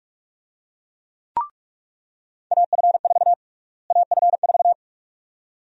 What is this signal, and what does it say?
Morse code, heard as a keyed tone, about 46 wpm, 700 Hz. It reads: AF4 AF4
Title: USA Call Sign Prefixes - Repeated 2x - 45wpm